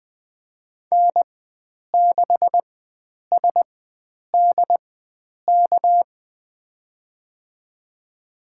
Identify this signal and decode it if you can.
Morse code, heard as a keyed tone, about 20 wpm, spaced as for 9 wpm, 700 Hz. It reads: N6SDK
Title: USA Call Signs - 5 Characters - 4x Character Spacing - 20wpm